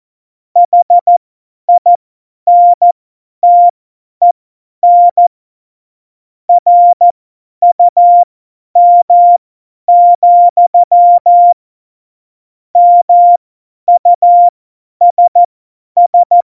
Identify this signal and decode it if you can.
Morse code, heard as a keyed tone, about 14 wpm, spaced as for 10 wpm, 700 Hz. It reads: HINTEN RUM, MUSS
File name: Morsekurs / Witze / Witz 28 April 2020, 50 F70